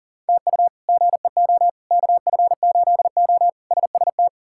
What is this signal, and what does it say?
Morse code, heard as a keyed tone, about 40 wpm, 700 Hz. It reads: TU GEO KF8O SST
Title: K1USN SST QSO Practice - 40wpm